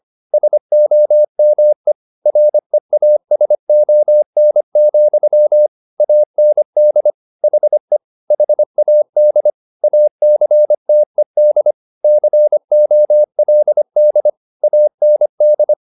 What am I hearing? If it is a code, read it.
Morse code, heard as a keyed tone, about 25 wpm, 600 Hz. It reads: SOME REASON, AND HE HAD ACTED COLD AND